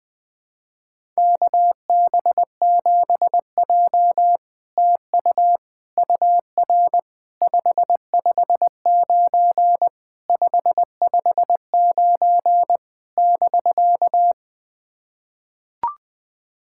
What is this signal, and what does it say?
Morse code, heard as a keyed tone, about 20 wpm, 700 Hz. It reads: KB7J TU UR 559 559 <BK>
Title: SOTA QSO Practice 20wpm